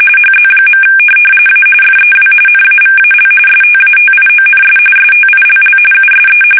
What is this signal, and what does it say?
RTTY, about 45.45 baud. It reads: HZ. A FEW NAVAL STATIONS STILL USE
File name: fsk850 / teletype-RTTY